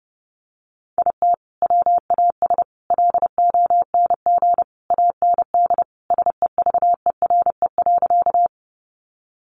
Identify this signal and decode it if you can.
Morse code, heard as a keyed tone, about 30 wpm, 700 Hz. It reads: IT WAS LONG AND SEVERE.
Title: Sentences from Top 1400 Words 30wpm